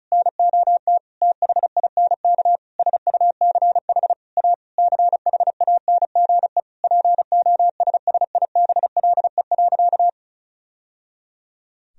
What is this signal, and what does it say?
Morse code, heard as a keyed tone, about 36 wpm, 700 Hz. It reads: NOT THINK SUCH A CHANGE POSSIBLE.
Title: Sentences from Top 1000 Words - Review Part 1 - 35wpm